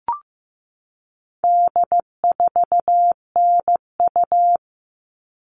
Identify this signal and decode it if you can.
Morse code, heard as a keyed tone, about 15 wpm, 700 Hz. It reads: D4NU
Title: All Cut Numbers - 4-Digit 15wpm